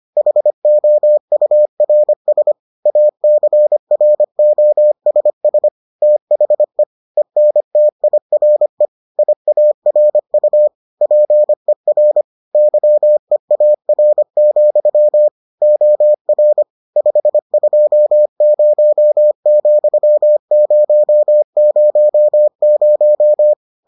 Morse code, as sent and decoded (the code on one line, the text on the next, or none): .... --- ..- .-. ... / .- -.-. .-. --- ... ... / - .... / . -. - .. .-. / .. .- .-. ..- / .--. .-. / -.-- . .- .-. --..-- / --- .-. / ..... ..--- ----- --..-- ----- ----- -----
HOURS ACROSS THE ENTIRE IARU PER YEAR, OR 520,000